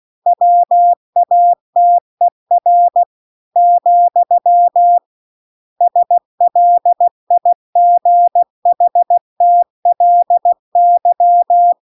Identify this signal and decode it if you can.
Morse code, heard as a keyed tone, about 16 wpm, 700 Hz. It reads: WATER , SLIGHTLY
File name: CW-The-Big-Twoheaded-River-1-16WPM